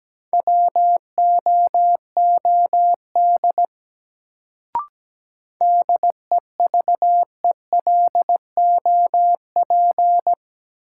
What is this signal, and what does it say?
Morse code, heard as a keyed tone, about 17 wpm, 700 Hz. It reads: WOOD DEVELOP
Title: Sentences from Top 1000 Words - Review Part 2 - New Word Drill - 17wpm